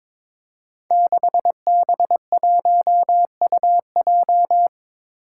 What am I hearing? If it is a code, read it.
Morse code, heard as a keyed tone, about 22 wpm, 700 Hz. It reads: 6B1UJ
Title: Five Challenging Letter-Number Combinations 22wpm